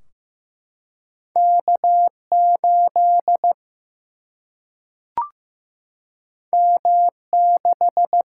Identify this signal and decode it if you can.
Morse code, heard as a keyed tone, about 15 wpm, 700 Hz. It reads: K8 M6
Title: Single-Letter Single-Number 15wpm